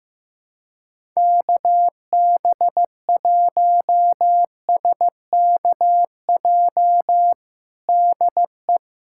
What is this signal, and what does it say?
Morse code, heard as a keyed tone, about 15 wpm, 700 Hz. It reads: KB1SKJ DE
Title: USA Callsign DE Callsign 15wpm